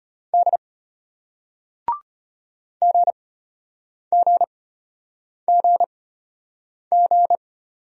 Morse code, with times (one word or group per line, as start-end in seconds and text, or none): 0.3-0.6 D
2.8-7.4 GGGG